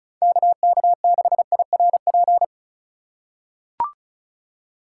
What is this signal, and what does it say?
Morse code, heard as a keyed tone, about 34 wpm, 700 Hz. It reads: KK6IRP
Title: USA Call Signs - 6 Characters - 3x Character Spacing - 35wpm